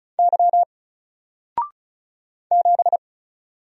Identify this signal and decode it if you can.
Morse code, heard as a keyed tone, about 34 wpm, 700 Hz. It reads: Y 7
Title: Single Letter-Number 35wpm